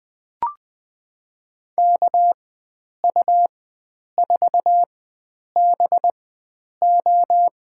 Morse code, heard as a keyed tone, about 20 wpm, spaced as for 9 wpm, 700 Hz. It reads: KU4BO
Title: USA Call Signs - 5 Characters - 4x Character Spacing - 20wpm